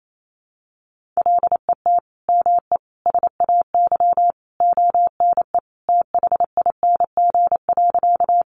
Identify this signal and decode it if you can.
Morse code, heard as a keyed tone, about 28 wpm, 700 Hz. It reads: LET ME SAY ONE THING.
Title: Sentences from Top 300 Words 28wpm